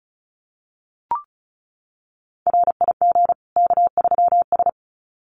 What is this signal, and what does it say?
Morse code, heard as a keyed tone, about 34 wpm, 700 Hz. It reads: RIG K3S
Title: Common QSO Phrases 35wpm